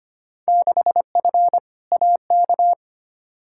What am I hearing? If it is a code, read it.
Morse code, heard as a keyed tone, about 25 wpm, 700 Hz. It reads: 6F AK
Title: Field Day Exchanges Encore - 25wpm